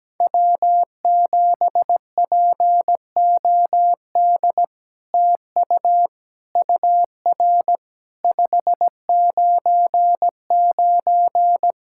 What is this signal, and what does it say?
Morse code, heard as a keyed tone, about 17 wpm, 700 Hz. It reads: W7POD TU UR 599
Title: SOTA QSO Practice 17wpm